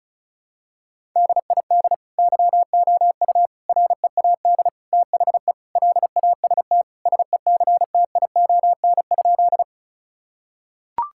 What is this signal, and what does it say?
Morse code, heard as a keyed tone, about 36 wpm, 700 Hz. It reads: DID YOU READ THE LAST SECTION?
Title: Sentences from Top 1000 Words - Review Part 5 - 35wpm